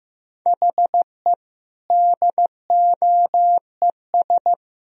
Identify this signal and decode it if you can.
Morse code, heard as a keyed tone, about 15 wpm, 700 Hz. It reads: HE DOES
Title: Speed-Racing - Sets of 5 Words 15wpm